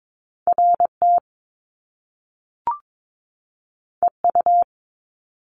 Morse code, heard as a keyed tone, about 22 wpm, 700 Hz. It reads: RT EU
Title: Top Two Letter Combinations in Words 22wpm